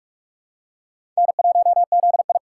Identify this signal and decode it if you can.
Morse code, heard as a keyed tone, about 42 wpm, 700 Hz. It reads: N1ZI
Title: USA Call Signs - 4 Characters - 6x Character Spacing - 45wpm